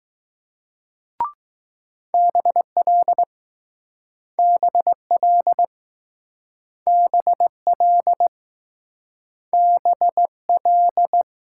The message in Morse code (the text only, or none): BL BL BL B L